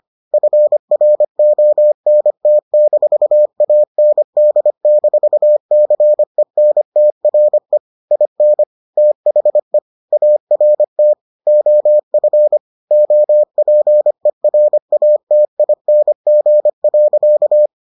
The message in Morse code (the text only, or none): FRONT-AND-CENTRE IN THE ART OF OPERATING.